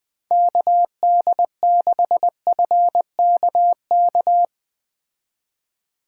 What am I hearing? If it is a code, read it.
Morse code, heard as a keyed tone, about 20 wpm, 700 Hz. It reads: KD6FKK